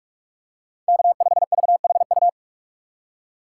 Morse code, heard as a keyed tone, about 48 wpm, 700 Hz. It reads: K5VHU
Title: USA Call Signs - 5 Characters - 4x Character Spacing - 45wpm